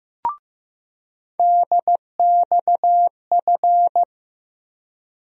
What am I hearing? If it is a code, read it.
Morse code, heard as a keyed tone, about 15 wpm, 700 Hz. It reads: DXF
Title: USA Call Sign Postfixes - 3 Letters 15wpm